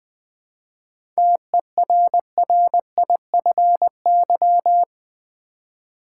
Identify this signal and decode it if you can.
Morse code, heard as a keyed tone, about 20 wpm, 700 Hz. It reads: TERRIFY
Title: 7 Letter Words - First 3 Letters Given - 20wpm